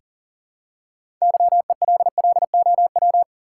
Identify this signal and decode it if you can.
Morse code, heard as a keyed tone, about 40 wpm, 700 Hz. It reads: YELLOW
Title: Six Letter Words - 40wpm